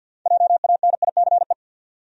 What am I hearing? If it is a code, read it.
Morse code, heard as a keyed tone, about 47 wpm, 700 Hz. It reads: JANICE
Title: USA Common Names - 50wpm